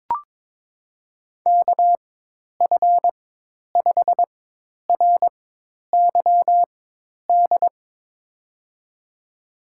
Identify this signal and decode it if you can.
Morse code, heard as a keyed tone, about 22 wpm, spaced as for 10 wpm, 700 Hz. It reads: KF5RYD
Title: USA Call Signs - 6 Characters - 4x Character Spacing - 22wpm